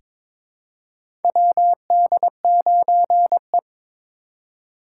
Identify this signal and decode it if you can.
Morse code, heard as a keyed tone, about 22 wpm, 700 Hz. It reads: WD9E
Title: USA Call Signs - 4 Characters - 2x Character Spacing - 22wpm